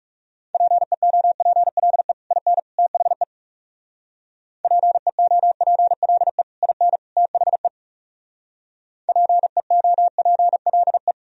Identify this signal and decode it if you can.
Morse code, heard as a keyed tone, about 40 wpm, 700 Hz. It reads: PEOPLE IN THE PEOPLE IN THE PEOPLE